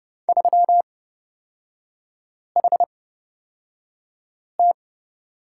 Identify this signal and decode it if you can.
Morse code, heard as a keyed tone, about 29 wpm, spaced as for 5 wpm, 700 Hz. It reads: 3HT
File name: Lesson 37e - Letter x - Three Characters 30wpm